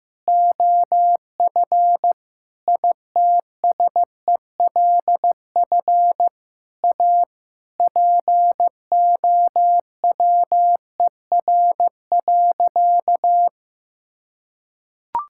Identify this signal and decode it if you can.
Morse code, heard as a keyed tone, about 15 wpm, 700 Hz. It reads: OF ITSELF A POWER.
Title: Sentences from Top 1000 Words - Review Part 5 - 15wpm